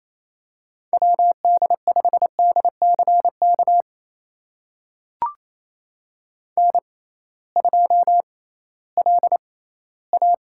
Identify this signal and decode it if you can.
Morse code, heard as a keyed tone, about 28 wpm, spaced as for 16 wpm, 700 Hz. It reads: WD5DCK N2LA
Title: USA Call Signs - 6x Character Spacing - 28wpm